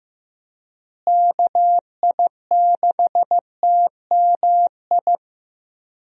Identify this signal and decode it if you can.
Morse code, heard as a keyed tone, about 16 wpm, 700 Hz. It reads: KI6TMI
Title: USA Call Signs - 6 Characters - Repeated 3x - 15wpm